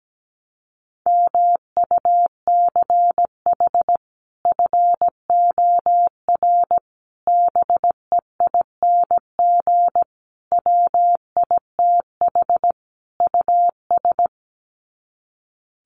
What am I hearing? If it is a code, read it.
Morse code, heard as a keyed tone, about 17 wpm, 700 Hz. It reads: MUCH FOR BEING WITH US
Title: Sets of 5 Words 17wpm